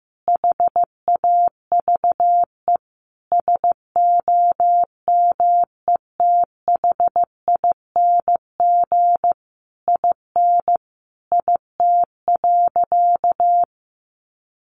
Morse code, heard as a keyed tone, about 15 wpm, 700 Hz. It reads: HAVE SOMETHING IN IT.
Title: Sentences from Top 1100 Words 15wpm